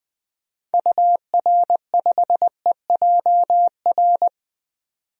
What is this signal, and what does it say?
Morse code, heard as a keyed tone, about 20 wpm, 700 Hz. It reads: UR5EJR
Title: DXCC-100 Call Signs - 6 Characters - 20wpm